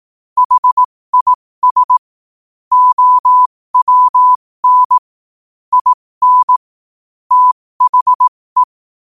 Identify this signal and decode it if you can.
Morse code, heard as a keyed tone, about 18 wpm, spaced as for 15 wpm, 1000 Hz. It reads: HIS OWN IN THE
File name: qotdfiles / 15wpm